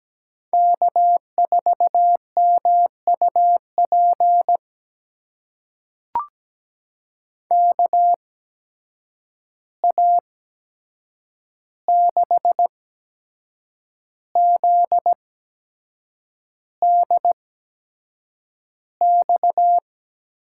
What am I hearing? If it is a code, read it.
Morse code, heard as a keyed tone, about 17 wpm, spaced as for 7 wpm, 700 Hz. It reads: K4MUP KA6ZDX
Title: USA Call Signs - 8x Character Spacing - 17wpm